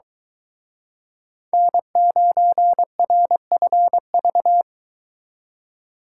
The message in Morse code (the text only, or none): N9RFV